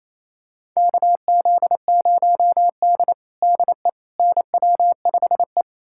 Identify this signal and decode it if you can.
Morse code, heard as a keyed tone, about 27 wpm, 700 Hz. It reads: KZ0D DE NW5E
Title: USA Callsign DE Callsign - 4 Characters - 28wpm